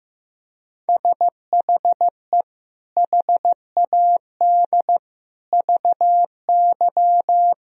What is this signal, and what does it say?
Morse code, heard as a keyed tone, about 15 wpm, 700 Hz. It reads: SHE HAD VY
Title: Sentences from Top 400 Words and Abbreviations 15wpm